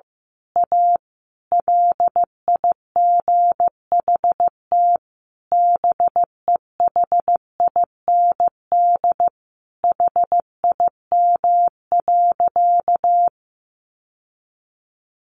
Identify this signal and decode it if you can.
Morse code, heard as a keyed tone, about 15 wpm, 700 Hz. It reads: A LIGHT BEHIND HIM.
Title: Sentences from Top 1000 Words 15wpm